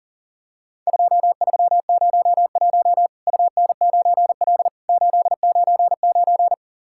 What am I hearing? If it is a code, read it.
Morse code, heard as a keyed tone, about 37 wpm, 700 Hz. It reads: 2301 UN9L 899
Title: Worked All Europe DX Contest - QTC Exchange - 40wpm